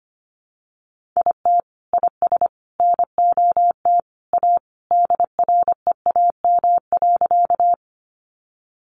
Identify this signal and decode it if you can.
Morse code, heard as a keyed tone, about 25 wpm, 700 Hz. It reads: IT IS NOT A DREAM.
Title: Sentences from Top 1500 Words 25wpm